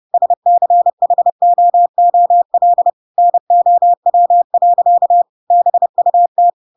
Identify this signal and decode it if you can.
Morse code, heard as a keyed tone, about 30 wpm, 700 Hz. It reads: SCHOOL NOW. BUT
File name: Pinocchio / Pinocchio ch8 30 30 ik1hge